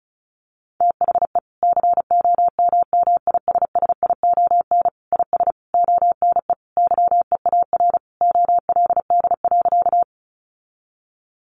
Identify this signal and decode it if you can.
Morse code, heard as a keyed tone, about 34 wpm, 700 Hz. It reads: THE COMMISSION IS ONE YEAR OLD.
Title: Sentences from Top 1400 Words 35wpm